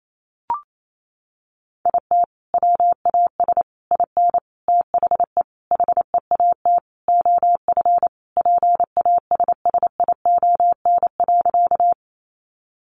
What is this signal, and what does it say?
Morse code, heard as a keyed tone, about 28 wpm, 700 Hz. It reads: IT WAS IN THE HEAT OF PASSION.
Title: Sentences from Top 1300 Words 28wpm